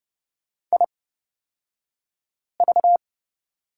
Morse code, heard as a keyed tone, about 30 wpm, spaced as for 5 wpm, 700 Hz. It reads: IV
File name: Lesson 38d - Letter q - Two Characters 30wpm